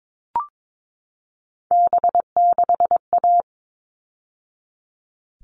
Morse code, heard as a keyed tone, about 22 wpm, 700 Hz. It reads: B6A
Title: All Cut Numbers - 3-Digit 22wpm